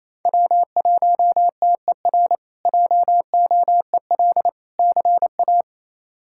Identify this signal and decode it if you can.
Morse code, heard as a keyed tone, about 27 wpm, 700 Hz. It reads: W1TER JOEL CA
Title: K1USN SST QSO Practice - 28wpm